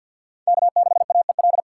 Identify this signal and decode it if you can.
Morse code, heard as a keyed tone, about 47 wpm, 700 Hz. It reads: K6AEL
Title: USA Call Signs - 5 Characters - 6x Character Spacing - 50wpm